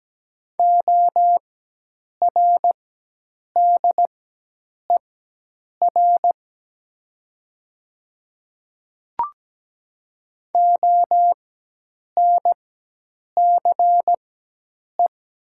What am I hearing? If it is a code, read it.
Morse code, heard as a keyed tone, about 17 wpm, spaced as for 8 wpm, 700 Hz. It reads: ORDER ONCE
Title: 4x Character Spacing - Top 500 Words - 17wpm